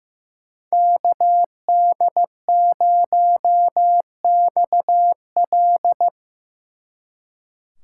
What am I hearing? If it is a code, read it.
Morse code, heard as a keyed tone, about 18 wpm, 700 Hz. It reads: KD0XL
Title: Speed Racing - USA 2x2 Call Signs 15wpm